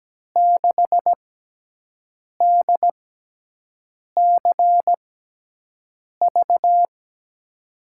Morse code, heard as a keyed tone, about 17 wpm, spaced as for 6 wpm, 700 Hz. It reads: 6DCV
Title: USA Call Signs - 5 Characters - 6x Character Spacing - 17wpm